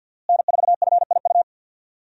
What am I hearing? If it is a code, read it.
Morse code, heard as a keyed tone, about 52 wpm, 700 Hz. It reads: N4FIU
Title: USA Call Signs - 5 Characters - Repeated 2x - 50wpm